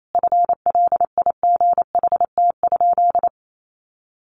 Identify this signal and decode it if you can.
Morse code, heard as a keyed tone, about 28 wpm, 700 Hz. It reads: FLIGHT?
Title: Sentences from Top 1000 Words 28wpm